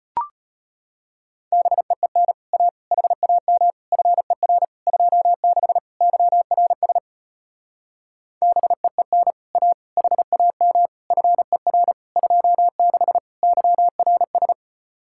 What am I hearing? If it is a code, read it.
Morse code, heard as a keyed tone, about 33 wpm, 700 Hz. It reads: BEEN A HAM FER 26 YRS BEEN A HAM FER 26 YRS